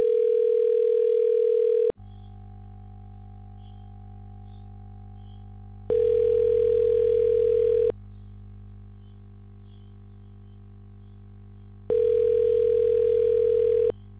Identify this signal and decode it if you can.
Ringback tone (North America), 440 + 480 Hz.